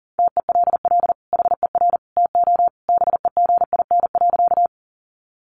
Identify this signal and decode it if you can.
Morse code, heard as a keyed tone, about 40 wpm, 700 Hz. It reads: TELL HER TO BEGIN.
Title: Sentences from Top 300 Words 40wpm